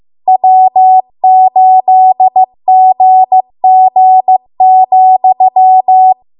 Morse code, heard as a keyed tone, about 15 wpm, 750 Hz. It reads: W8GG,
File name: Archive / 15 WPM / 15WPM